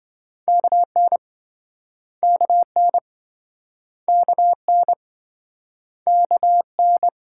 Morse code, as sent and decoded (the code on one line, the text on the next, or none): -.- -. / -.- -. / -.- -. / -.- -.
KN KN KN KN